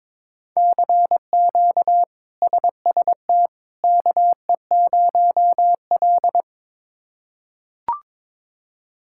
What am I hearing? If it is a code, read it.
Morse code, heard as a keyed tone, about 22 wpm, 700 Hz. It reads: CQ SST KE0L
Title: K1USN SST QSO Practice - 22wpm